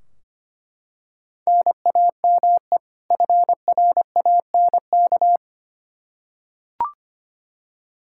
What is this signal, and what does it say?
Morse code, heard as a keyed tone, about 25 wpm, 700 Hz. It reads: NAME FRANK
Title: Practice QSO - Exchange 1 - Beginning - 25wpm